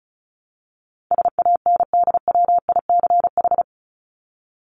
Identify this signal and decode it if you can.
Morse code, heard as a keyed tone, about 34 wpm, 700 Hz. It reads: SANDWICH
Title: Top 1751-2000 Words 35wpm